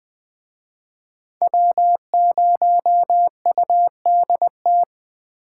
Morse code, heard as a keyed tone, about 20 wpm, 700 Hz. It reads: W0UDT
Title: Speed Racing - USA 1x3 Call Signs 17wpm